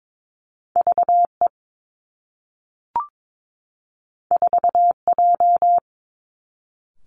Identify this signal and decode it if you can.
Morse code, heard as a keyed tone, about 22 wpm, 700 Hz. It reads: VE 4J
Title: Two Challenging Letter-Number Combinations 22wpm